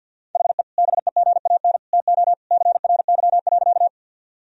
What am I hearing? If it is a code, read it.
Morse code, heard as a keyed tone, about 49 wpm, 700 Hz. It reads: HE BEGAN TO CRY.
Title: Sentences from Top 1000 Words - Review Part 2 - 50wpm